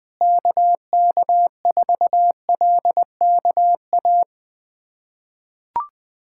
Morse code, heard as a keyed tone, about 20 wpm, 700 Hz. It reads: KK4LKA
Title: USA Callsign DE Callsign 20wpm